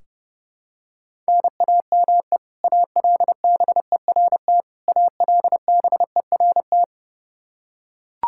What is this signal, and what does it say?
Morse code, heard as a keyed tone, about 30 wpm, 700 Hz. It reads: NAME ALBERT ALBERT
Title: Practice QSO - Exchange 1 - Beginning - 30wpm